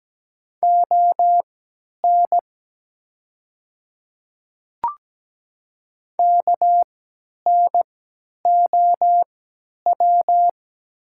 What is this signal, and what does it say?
Morse code, heard as a keyed tone, about 17 wpm, spaced as for 10 wpm, 700 Hz. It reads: ON KNOW
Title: 3x Character Spacing - Top 100 Words - 17wpm